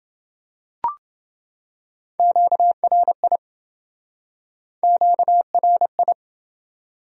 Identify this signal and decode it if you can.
Morse code, heard as a keyed tone, about 29 wpm, 700 Hz. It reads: QRI QRI